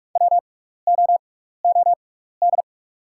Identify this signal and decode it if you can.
Morse code, heard as a keyed tone, about 46 wpm, spaced as for 15 wpm, 700 Hz. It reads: WOOD
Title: 6x Character Spacing - Top 500 Words - 45wpm